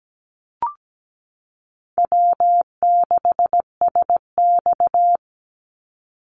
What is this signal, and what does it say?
Morse code, heard as a keyed tone, about 17 wpm, 700 Hz. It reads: W6SX
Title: Easy Call Signs - 17wpm